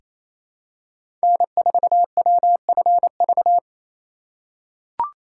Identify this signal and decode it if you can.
Morse code, heard as a keyed tone, about 28 wpm, 700 Hz. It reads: N4WFV
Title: USA Call Signs - 3x Character Spacing - 28wpm